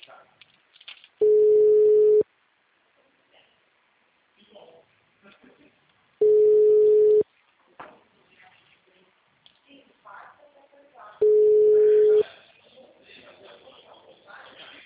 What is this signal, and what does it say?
Ringback tone (Europe, 425 Hz), with speech.
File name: out- -2005- - - .24